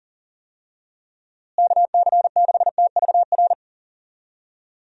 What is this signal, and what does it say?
Morse code, heard as a keyed tone, about 40 wpm, 700 Hz. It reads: KC6TVR
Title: USA Call Signs - 6 Characters - Repeated 3x - 40wpm